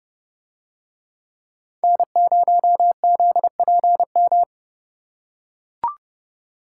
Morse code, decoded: N0ZPM

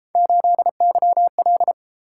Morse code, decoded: 8YL

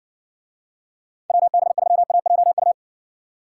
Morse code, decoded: WB3AJU

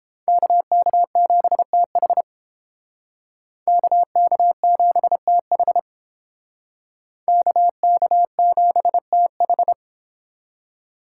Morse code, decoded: KK7TH KK7TH KK7TH

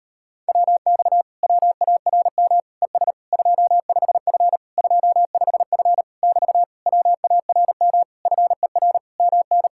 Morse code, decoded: WX WARM ES 25F 25F <BT> WARM FER MN

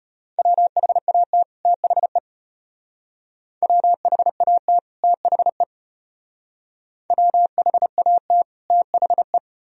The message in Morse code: WHAT THE WHAT THE WHAT THE

